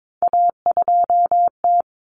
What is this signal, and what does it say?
Morse code, heard as a keyed tone, about 22 wpm, 700 Hz. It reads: A2T
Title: Common Cut Numbers - 3-Digit 22wpm